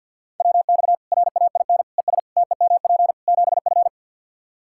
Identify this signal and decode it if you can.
Morse code, heard as a keyed tone, about 47 wpm, 700 Hz. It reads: WX RAIN ES TEMP 7F
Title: Practice QSO - Exchange 2 - Info - 50wpm